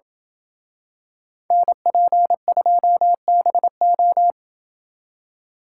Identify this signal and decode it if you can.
Morse code, heard as a keyed tone, about 26 wpm, 700 Hz. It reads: NP2BO